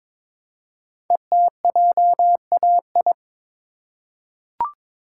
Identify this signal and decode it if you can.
Morse code, heard as a keyed tone, about 22 wpm, 700 Hz. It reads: ETJAI
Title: Code Groups - 5 Letters - 22wpm